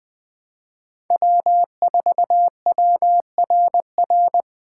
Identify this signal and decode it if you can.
Morse code, heard as a keyed tone, about 21 wpm, 700 Hz. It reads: W4WRR